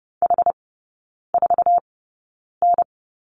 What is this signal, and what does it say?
Morse code, heard as a keyed tone, about 30 wpm, spaced as for 9 wpm, 700 Hz. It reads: H4N
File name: Lesson 18e - Letter m - Three Characters 30wpm